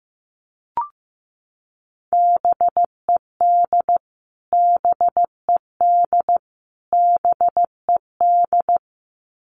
Morse code, heard as a keyed tone, about 15 wpm, 700 Hz. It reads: BED BED BED
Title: Top 500 Words Repeated 15wpm